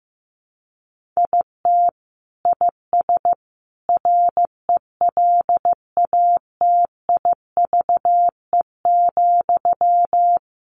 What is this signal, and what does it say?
Morse code, heard as a keyed tone, about 15 wpm, 700 Hz. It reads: IT IS RELATIVE,
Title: Sentences from Top 1400 Words 15wpm